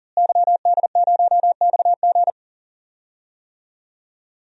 Morse code, heard as a keyed tone, about 40 wpm, 700 Hz. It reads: YD0XG
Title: DXCC-100 Call Signs - 5 Characters - Repeated 2x - 40wpm